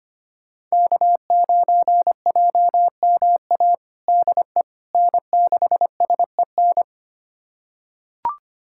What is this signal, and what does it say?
Morse code, heard as a keyed tone, about 25 wpm, 700 Hz. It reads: K9JMA DE N6SEN
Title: USA Callsign DE Callsign - 5 Characters - 25wpm